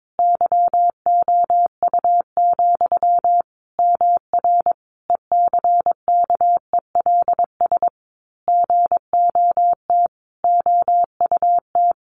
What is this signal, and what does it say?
Morse code, heard as a keyed tone, about 22 wpm, 700 Hz. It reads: YOU, MR ECKELS GOT OUT
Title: A Sound Of Thunder 22wpm